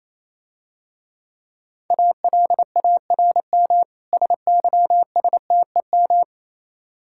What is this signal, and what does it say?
Morse code, heard as a keyed tone, about 27 wpm, 700 Hz. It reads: ALARM SYSTEM
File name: Easy Words - Speed-Racing with Hints - 22wpm